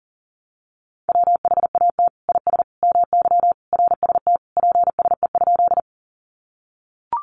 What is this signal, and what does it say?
Morse code, heard as a keyed tone, about 40 wpm, 700 Hz. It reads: WHAT IS MY RST PSE?
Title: Common QSO Phrases 40wpm